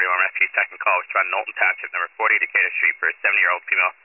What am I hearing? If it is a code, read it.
Emergency Alert System (SAME) tones, heard as harsh buzzing bursts, with speech. It is heard from 0.3 to 3.2 s.